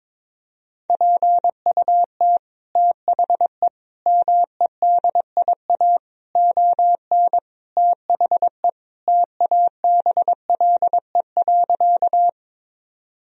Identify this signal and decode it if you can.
Morse code, heard as a keyed tone, about 22 wpm, 700 Hz. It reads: PUT THE MEDIA ON THE TABLE.
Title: Sentences from Top 1000 Words - Review Part 4 - 22wpm